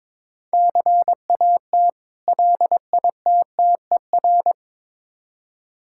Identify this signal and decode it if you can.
Morse code, heard as a keyed tone, about 22 wpm, 690 Hz. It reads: CAT LITTER
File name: Difficult Words with Hints - 22wpm